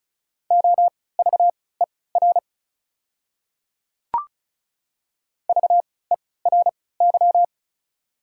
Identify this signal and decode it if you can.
Morse code, heard as a keyed tone, about 36 wpm, spaced as for 20 wpm, 690 Hz. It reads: OVER VERY
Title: 3x Character Spacing - Top 200 Words - 35wpm